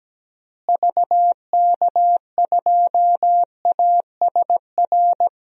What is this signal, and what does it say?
Morse code, heard as a keyed tone, about 17 wpm, 700 Hz. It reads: VK2ASR
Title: DXCC-100 Call Signs - 6 Characters - Repeated 2x - 17wpm